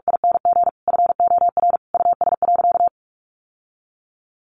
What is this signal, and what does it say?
Morse code, heard as a keyed tone, about 44 wpm, 700 Hz. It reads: ING FOR US.
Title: Sentences from Top 1200 Words 45wpm